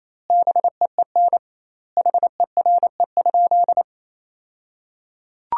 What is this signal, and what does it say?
Morse code, heard as a keyed tone, about 28 wpm, 700 Hz. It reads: BEEN HERE?
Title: 2x Word-Spacing - Sentences from Top 200 Words - 28wpm